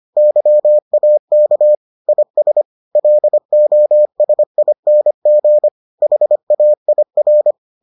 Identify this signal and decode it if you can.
Morse code, heard as a keyed tone, about 25 wpm, 600 Hz. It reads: YAK IS LOSING HAIR